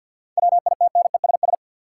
Morse code, heard as a keyed tone, about 50 wpm, 700 Hz. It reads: WITNESS
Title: Seven Letter Words - 50wpm